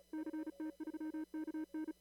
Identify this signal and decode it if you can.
Morse code, heard as a keyed tone, about 31 wpm, 310 Hz. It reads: CT3KN
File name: CW / kt5j wpx cw-op-k5pi